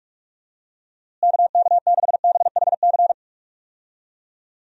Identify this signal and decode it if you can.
Morse code, heard as a keyed tone, about 46 wpm, 700 Hz. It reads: KK6BHC
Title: USA Call Signs - 6 Characters - Repeated 2x - 45wpm